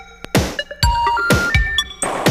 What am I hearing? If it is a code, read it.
Special information tone, with noise.